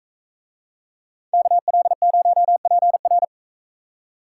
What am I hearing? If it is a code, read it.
Morse code, heard as a keyed tone, about 44 wpm, 700 Hz. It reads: KL0PR